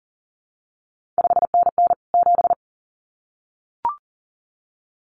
Morse code, heard as a keyed tone, about 41 wpm, 700 Hz. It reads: 5NN 7
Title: CQ WW DX Contest Exchange Encore - 40wpm